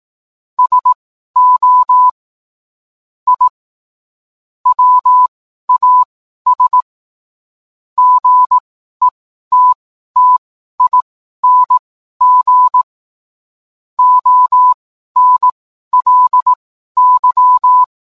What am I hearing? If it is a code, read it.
Morse code, heard as a keyed tone, about 18 wpm, spaced as for 12 wpm, 1000 Hz. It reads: SO I WAS GETTING ONLY